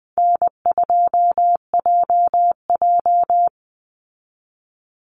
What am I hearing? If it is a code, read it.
Morse code, heard as a keyed tone, about 20 wpm, 700 Hz. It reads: N2JJ
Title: Call Signs - 20wpm